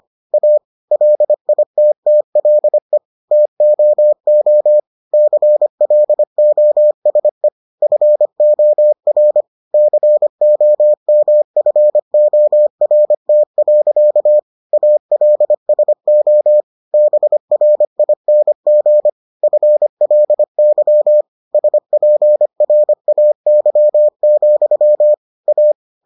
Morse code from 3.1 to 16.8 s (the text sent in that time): TOO CLOSE FOR COMFORT. ALSO